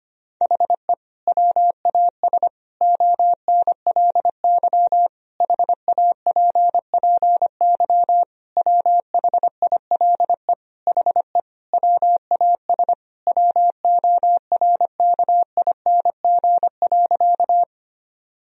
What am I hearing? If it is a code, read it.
Morse code, heard as a keyed tone, about 25 wpm, 700 Hz. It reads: HE WAS ONLY HAPPY WHILE HE WAS WORKING.
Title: Sentences from Top 1000 Words - Review Part 4 - 25wpm